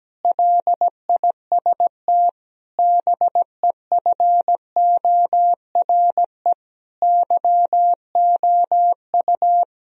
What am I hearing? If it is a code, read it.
Morse code, heard as a keyed tone, about 17 wpm, 700 Hz. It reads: LIST BEFORE YOU
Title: Sentences from Top 1000 Words - Review Part 2 - 17wpm